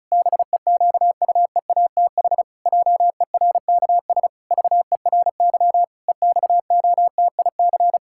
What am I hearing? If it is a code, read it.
Morse code, heard as a keyed tone, about 36 wpm, 700 Hz. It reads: BEQUEATH JERKS VERY EXOTIC
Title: Pangram Challenge v2 - 35wpm